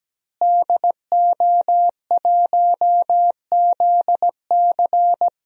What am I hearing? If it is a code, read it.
Morse code, heard as a keyed tone, about 17 wpm, 700 Hz. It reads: DO1ZC
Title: DXCC-100 Call Signs - 5 Characters - Repeated 3x - 17wpm